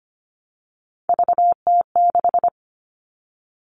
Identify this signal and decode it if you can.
Morse code, heard as a keyed tone, about 25 wpm, 700 Hz. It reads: VT6
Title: All Cut Numbers - 3-Digit 25wpm